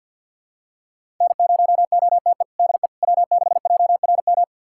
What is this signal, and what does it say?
Morse code, heard as a keyed tone, about 49 wpm, 700 Hz. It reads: N0OTE DE W6JRM
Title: USA Callsign DE Callsign - 5 Characters - 50wpm